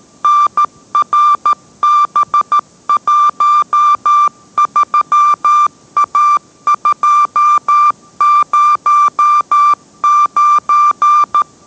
Morse code, heard as a keyed tone, about 13 wpm, 1180 Hz. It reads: NRB13A209